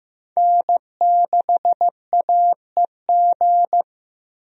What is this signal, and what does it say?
Morse code, heard as a keyed tone, about 15 wpm, 700 Hz. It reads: N6AEG
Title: USA Call Signs - 5 Characters - Repeated 2x - 15wpm